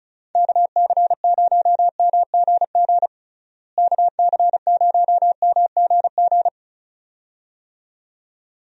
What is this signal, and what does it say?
Morse code, heard as a keyed tone, about 36 wpm, 700 Hz. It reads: KC0MGG KC0MGG
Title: USA Call Signs - 6 Characters - Repeated 2x - 35wpm